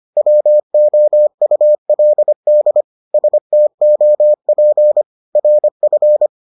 Morse code, heard as a keyed tone, about 25 wpm, 600 Hz. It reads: WOULD STOP RF